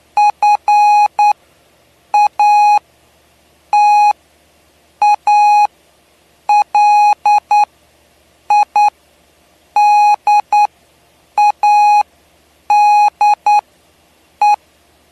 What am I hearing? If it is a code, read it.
Morse code, heard as a keyed tone, about 9 wpm, spaced as for 7 wpm, 830 Hz. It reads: FATALIDADE
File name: cw / py2lcd / 07-PY2LCD-Li��o 7